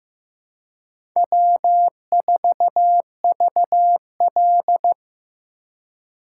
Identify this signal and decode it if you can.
Morse code, heard as a keyed tone, about 15 wpm, 700 Hz. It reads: W4VL
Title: USA Call Signs - 4 Characters - 3x Character Spacing - 15wpm